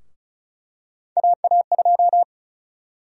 Morse code, heard as a keyed tone, about 34 wpm, 700 Hz. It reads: AA2
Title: USA Call Sign Prefixes - Repeated 2x - 35wpm